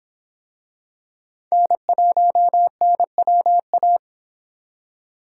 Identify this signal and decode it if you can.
Morse code, heard as a keyed tone, about 26 wpm, 700 Hz. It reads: N1NWA